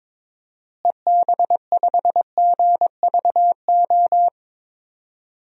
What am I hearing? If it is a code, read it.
Morse code, heard as a keyed tone, about 21 wpm, 700 Hz. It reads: EB5GVO
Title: DXCC-100 Call Signs - 6 Characters - Repeated 3x - 22wpm